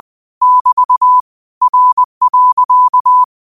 Morse code, heard as a keyed tone, about 20 wpm, 1000 Hz. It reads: <BT> R.
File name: qotdfiles / 20wpm